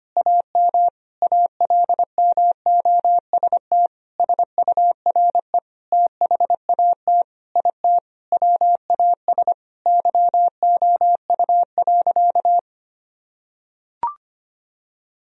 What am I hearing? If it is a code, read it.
Morse code, heard as a keyed tone, about 25 wpm, 700 Hz. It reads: AM ALMOST SURE THAT IT WAS YOU.
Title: Sentences from Top 1000 Words - Review Part 4 - 25wpm